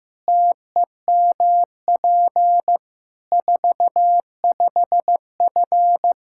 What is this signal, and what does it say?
Morse code, heard as a keyed tone, about 15 wpm, 700 Hz. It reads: TEMP 45F
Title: Practice QSO - Exchange 2 - Info - 15wpm